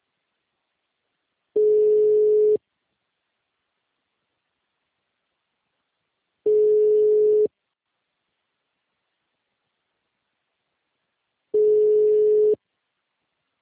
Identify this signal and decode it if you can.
Ringback tone (Europe, 425 Hz).